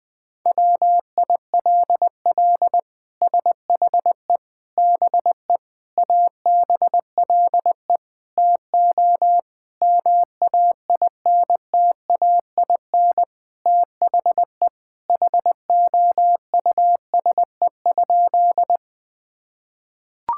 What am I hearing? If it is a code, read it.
Morse code, heard as a keyed tone, about 20 wpm, 700 Hz. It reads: WILL SHE BE ABLE TO MAINTAIN THE HOUSE?
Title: Sentences from Top 1000 Words - Review Part 5 - 20wpm